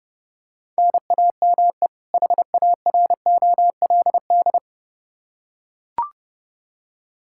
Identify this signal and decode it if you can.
Morse code, heard as a keyed tone, about 30 wpm, 700 Hz. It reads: NAME HAROLD
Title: Practice QSO - Exchange 1 - Beginning - 30wpm